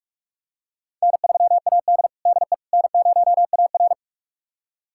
Morse code, heard as a keyed tone, about 47 wpm, 700 Hz. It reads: N3AD DE N0AR
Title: USA Callsign DE Callsign - 4 Characters - 45wpm